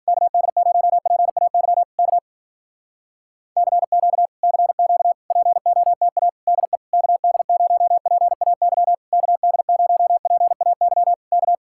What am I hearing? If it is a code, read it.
Morse code, heard as a keyed tone, about 58 wpm, 700 Hz. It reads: KD0PAY K CQ CQ POTA DE KD0PAY KD0PAY K